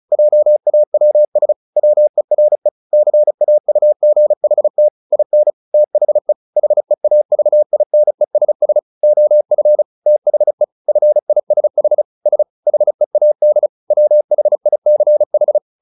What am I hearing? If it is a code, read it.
Morse code, heard as a keyed tone, about 34 wpm, 600 Hz. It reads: JAWS WERE CAUGHT IN THE HEAVINESS OF THE FISH S HEAD WHICH
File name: the old man and the sea 2 35wpm